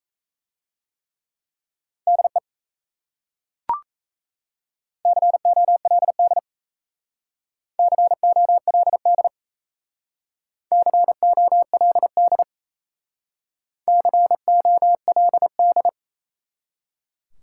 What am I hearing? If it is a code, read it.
Morse code, heard as a keyed tone, about 37 wpm, 700 Hz. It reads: DE COLD COLD COLD COLD